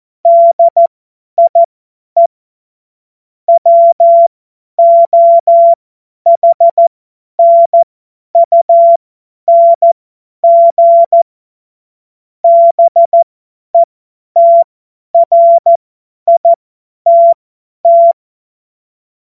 Morse code, heard as a keyed tone, about 14 wpm, spaced as for 10 wpm, 680 Hz. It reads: DIE WOHNUNG BETRITT
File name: Morsekurs / Witze / Witz 04. Januar 2022, 50 F70